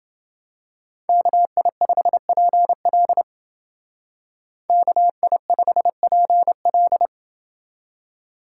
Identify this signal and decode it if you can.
Morse code, heard as a keyed tone, about 28 wpm, 700 Hz. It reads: KI5PL KI5PL